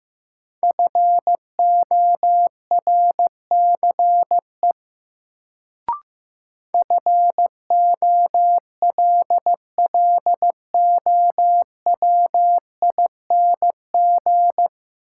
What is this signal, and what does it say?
Morse code, heard as a keyed tone, about 15 wpm, 700 Hz. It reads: FORCE FOLLOWING
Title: Sentences from Top 700 Words - New Word Drill - 15wpm